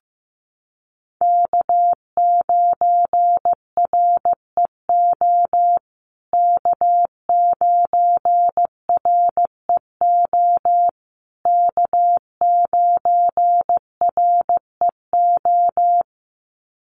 Morse code, heard as a keyed tone, about 15 wpm, 700 Hz. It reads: K9REO K9REO K9REO
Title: USA Call Signs Repeated 3x 15wpm